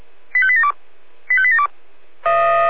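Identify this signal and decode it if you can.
Five-tone selective call (ZVEI), heard as quick runs of five tones, tones of about 70 ms. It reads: 85872 85872
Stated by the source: ZVEI five-tone selective call 85259 received by SDR.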